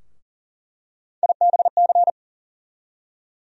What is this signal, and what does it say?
Morse code, heard as a keyed tone, about 40 wpm, 700 Hz. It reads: IBC
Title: USA Call Sign Postfixes - 3 Letters 40wpm